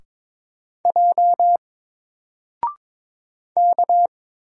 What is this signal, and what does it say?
Morse code, heard as a keyed tone, about 22 wpm, 700 Hz. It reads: J K
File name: Single-Letter-Number-22wpm